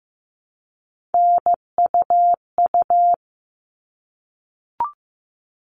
Morse code, heard as a keyed tone, about 15 wpm, 700 Hz. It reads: NUU
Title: All Cut Numbers - 3-Digit 15wpm